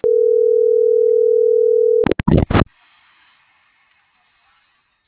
Ringback tone (North America), 440 + 480 Hz.